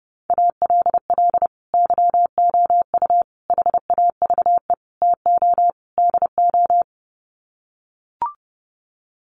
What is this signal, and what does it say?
Morse code, heard as a keyed tone, about 30 wpm, 700 Hz. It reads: ALL YOU HAVE TO DO
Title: Sets of 5 Words 30wpm